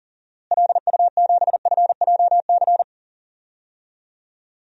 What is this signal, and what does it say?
Morse code, heard as a keyed tone, about 40 wpm, 700 Hz. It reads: LU7FJC
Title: DXCC-100 Call Signs - 6 Characters - Repeated 3x - 40wpm